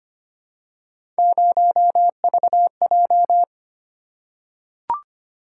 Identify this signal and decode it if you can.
Morse code, heard as a keyed tone, about 25 wpm, 700 Hz. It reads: 0VJ
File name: Three Letter-Number Combinations - 25wpm